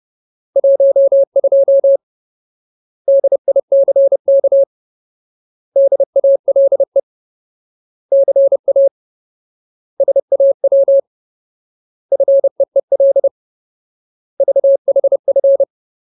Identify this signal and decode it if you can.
Morse code, heard as a keyed tone, about 30 wpm, 550 Hz. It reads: 12 DICK DALE CA SAW FEEL VHF
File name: Ultimate Word List / 30WPM Ultimate